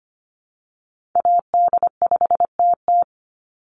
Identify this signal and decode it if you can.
Morse code, heard as a keyed tone, about 25 wpm, 700 Hz. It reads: AD5TT
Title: Call Signs - 25wpm